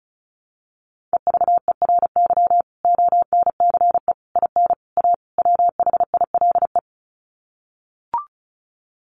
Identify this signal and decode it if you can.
Morse code, heard as a keyed tone, about 34 wpm, 700 Hz. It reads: EVERY ONCE IN A WHILE
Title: Sets of 5 Words 35wpm